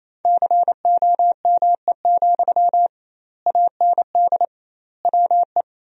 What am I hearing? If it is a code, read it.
Morse code, heard as a keyed tone, about 28 wpm, 700 Hz. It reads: COME, AND WE
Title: 2x Word-Spacing - Sentences from Top 500 Words - 28wpm